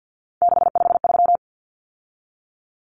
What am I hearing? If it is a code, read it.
Morse code, heard as a keyed tone, about 48 wpm, 700 Hz. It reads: 653
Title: Common Cut Numbers - 3-Digit 50wpm